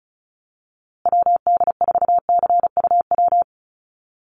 Morse code, heard as a keyed tone, about 34 wpm, 700 Hz. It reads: WD4CUW
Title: USA Call Signs Repeated 3x 35wpm